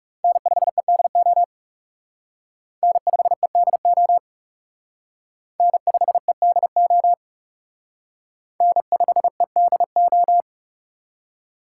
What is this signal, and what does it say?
Morse code, heard as a keyed tone, about 36 wpm, 700 Hz. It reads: N5EDO N5EDO N5EDO N5EDO
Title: Speed Racing - USA Call Signs 30wpm